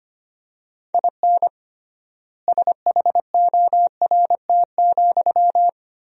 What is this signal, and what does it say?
Morse code, heard as a keyed tone, about 25 wpm, 700 Hz. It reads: IN SHORT,
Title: 3x Word-Spacing - Sentences from Top 400 Words - 25wpm